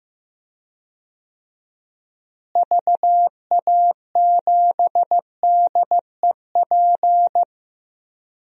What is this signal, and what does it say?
Morse code, heard as a keyed tone, about 15 wpm, 700 Hz. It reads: VA7DEP
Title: DXCC-100 Call Signs - 6 Characters - Repeated 3x - 15wpm